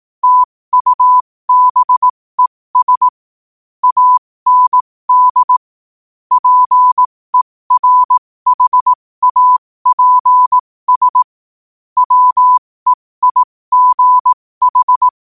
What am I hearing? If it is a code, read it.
Morse code, heard as a keyed tone, about 18 wpm, spaced as for 15 wpm, 1000 Hz. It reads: TUBES AND PERHAPS WEIGH
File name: qotdfiles / 15wpm